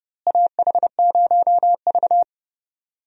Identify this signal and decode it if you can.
Morse code, heard as a keyed tone, about 30 wpm, 700 Hz. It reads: AH0V
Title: USA Call Signs - 4 Characters - Repeated 3x - 30wpm